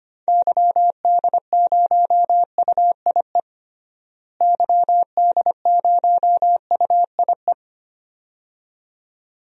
Morse code, heard as a keyed tone, about 25 wpm, 700 Hz. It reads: YD0UIE YD0UIE
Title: DXCC-100 Call Signs - 6 Characters - Repeated 2x - 25wpm